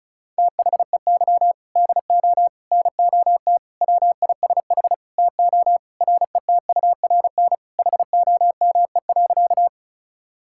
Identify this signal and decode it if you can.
Morse code, heard as a keyed tone, about 36 wpm, 700 Hz. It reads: THEY DO NOT WISH TO RETURN HOME.
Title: Sentences from Top 1600 Words 35wpm